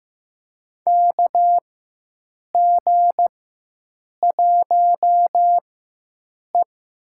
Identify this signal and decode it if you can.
Morse code, heard as a keyed tone, about 15 wpm, spaced as for 7 wpm, 700 Hz. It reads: KG1E
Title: USA Call Signs - 4 Characters - 4x Character Spacing - 15wpm